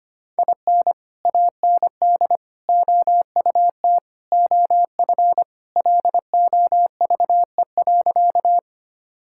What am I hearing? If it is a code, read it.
Morse code, heard as a keyed tone, about 25 wpm, 700 Hz. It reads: IN AND OUT OF LOVE.
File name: practice / 25wpm / Sentences with 4-Word Phrases - 25wpm